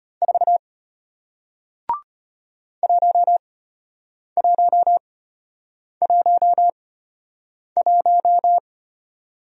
Morse code, heard as a keyed tone, about 34 wpm, spaced as for 8 wpm, 700 Hz. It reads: 4 1111